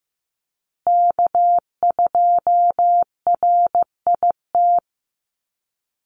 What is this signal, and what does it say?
Morse code, heard as a keyed tone, about 15 wpm, 700 Hz. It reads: K2RIT
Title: USA Call Signs Repeated 2x 15wpm